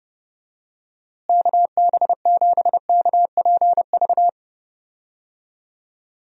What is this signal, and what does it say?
Morse code, heard as a keyed tone, about 30 wpm, 700 Hz. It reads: KB7KPV
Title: USA Call Signs - 6 Characters - Repeated 3x - 30wpm